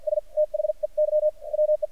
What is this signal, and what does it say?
Morse code, heard as a keyed tone, about 53 wpm, spaced as for 44 wpm, 610 Hz. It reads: S THEY WE
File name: archived / CRSnet